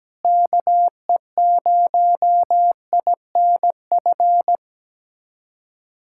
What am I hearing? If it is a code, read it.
Morse code, heard as a keyed tone, about 17 wpm, 700 Hz. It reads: KE0INF
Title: USA Callsign DE Callsign 17wpm